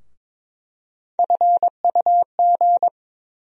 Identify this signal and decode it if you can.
Morse code, heard as a keyed tone, about 22 wpm, 700 Hz. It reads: FUG
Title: USA Call Sign Postfixes - 3 Letters 22wpm